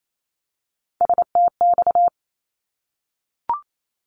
Morse code, heard as a keyed tone, about 28 wpm, 700 Hz. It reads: STX
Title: ARRL Sections 28wpm